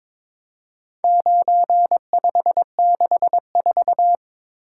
Morse code, heard as a keyed tone, about 22 wpm, 700 Hz. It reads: 9564
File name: practice / 22wpm / Four Letter-Number Combinations - 22wpm